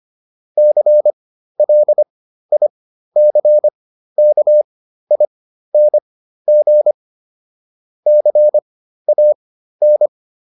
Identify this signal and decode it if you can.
Morse code, heard as a keyed tone, about 25 wpm, spaced as for 13 wpm, 600 Hz. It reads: CLICKING CAN